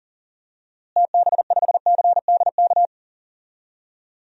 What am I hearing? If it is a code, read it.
Morse code, heard as a keyed tone, about 37 wpm, 700 Hz. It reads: TB5CDK
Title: DXCC-100 Call Signs - 6 Characters - Repeated 2x - 40wpm